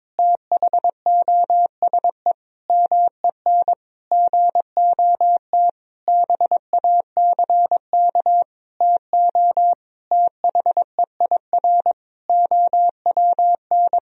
Morse code, heard as a keyed tone, about 22 wpm, 700 Hz. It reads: THOSE MEN GOT BACK TO THEIR OWN
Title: Sentences from Top 1000 Words - Review Part 1 - 22wpm